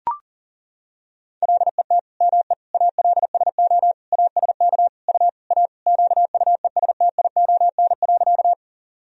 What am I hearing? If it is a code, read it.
Morse code, heard as a keyed tone, about 40 wpm, 700 Hz. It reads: LET ME ALSO ASK U A QUESTION.
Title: Sentences from Top 400 Words and Abbreviations 40wpm